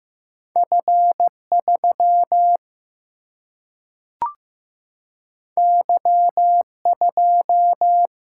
Morse code, heard as a keyed tone, about 15 wpm, 700 Hz. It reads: F 3 Y 2
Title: Single-Letter Single-Number 15wpm